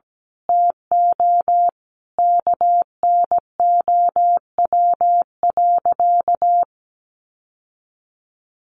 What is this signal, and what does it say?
Morse code, heard as a keyed tone, about 17 wpm, 700 Hz. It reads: TO KNOW.
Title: Sentences from Top 1200 Words 17wpm